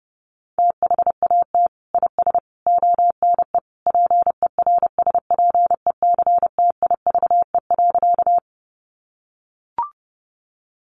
Morse code, heard as a keyed tone, about 30 wpm, 700 Hz. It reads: THAT IS ONE PERSPECTIVE.
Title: Sentences from Top 1200 Words 30wpm